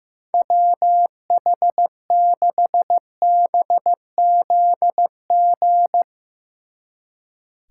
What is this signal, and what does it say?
Morse code, heard as a keyed tone, about 16 wpm, 700 Hz. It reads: WH6BZG
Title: Speed Racing - USA Call Signs 15wpm